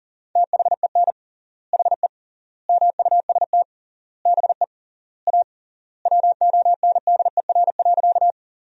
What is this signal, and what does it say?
Morse code, heard as a keyed tone, about 40 wpm, 700 Hz. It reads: THEN HE MUST BE A WONDER.
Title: 3x Word-Spacing - Sentences from Top 500 Words - 40wpm